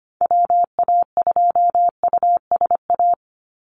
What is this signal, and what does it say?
Morse code, heard as a keyed tone, about 25 wpm, 700 Hz. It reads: WA2USA
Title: NA SOTA Activator Callsigns 25wpm